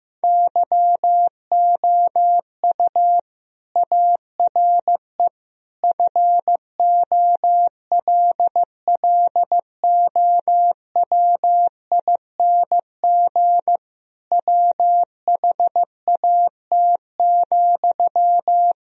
Morse code, heard as a keyed tone, about 15 wpm, 700 Hz. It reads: YOU ARE FOLLOWING WHAT,
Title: Sentences from Top 1000 Words - Review Part 4 - 15wpm